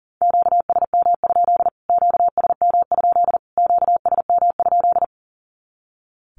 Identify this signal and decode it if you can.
Morse code, heard as a keyed tone, about 40 wpm, 700 Hz. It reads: QSM? QSM? QSM?
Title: Q-Codes Repeated 3x 40wpm